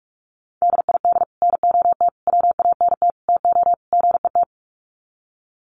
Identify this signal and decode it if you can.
Morse code, heard as a keyed tone, about 44 wpm, 700 Hz. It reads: DID NOT WANT TO GET
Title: Sets of 5 Words Encore2 45wpm